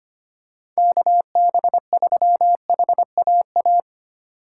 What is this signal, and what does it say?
Morse code, heard as a keyed tone, about 28 wpm, 700 Hz. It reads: KB3HAA
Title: Speed Racing - USA 2x3 Call Signs 25wpm